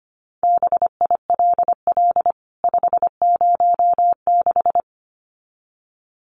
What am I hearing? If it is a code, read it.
Morse code, heard as a keyed tone, about 25 wpm, 700 Hz. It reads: BILL 506
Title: CWT Exchange 25wpm